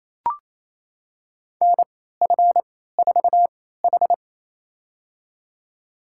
Morse code, heard as a keyed tone, about 28 wpm, spaced as for 16 wpm, 700 Hz. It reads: NF4H
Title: USA Call Signs - 4 Characters - 3x Character Spacing - 28wpm